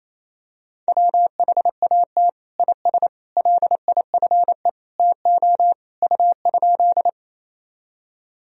Morse code, heard as a keyed tone, about 28 wpm, 700 Hz. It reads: WHAT IS LIFE TO U?
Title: Sentences from Top 300 Words and Abbreviations 28wpm